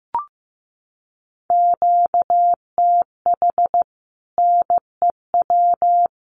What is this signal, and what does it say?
Morse code, heard as a keyed tone, about 15 wpm, 700 Hz. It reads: QTH NEW
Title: Common QSO Phrases 15wpm